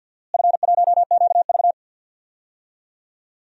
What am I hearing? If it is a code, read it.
Morse code, heard as a keyed tone, about 53 wpm, 700 Hz. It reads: F1QV
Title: DXCC-100 Call Signs - 4 Characters - Repeated 2x - 50wpm